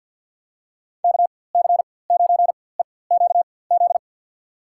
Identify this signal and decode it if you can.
Morse code, heard as a keyed tone, about 47 wpm, spaced as for 23 wpm, 700 Hz. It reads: KC9EQZ
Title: USA Call Signs - 6 Characters - 4x Character Spacing - 50wpm